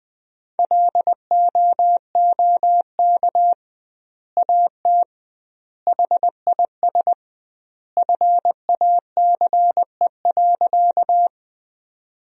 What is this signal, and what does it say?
Morse code, heard as a keyed tone, about 20 wpm, 700 Hz. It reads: LOOK AT HIS FACE.
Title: 2x Word-Spacing - Sentences from Top 300 Words - 20wpm